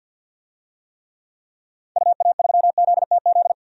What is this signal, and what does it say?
Morse code, heard as a keyed tone, about 47 wpm, 700 Hz. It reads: UA3ZTZ
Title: DXCC-100 Call Signs - 6 Characters - Repeated 2x - 50wpm